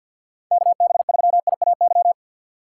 Morse code, heard as a keyed tone, about 52 wpm, 700 Hz. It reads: KB3IAY
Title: USA Call Signs - 6 Characters - 4x Character Spacing - 50wpm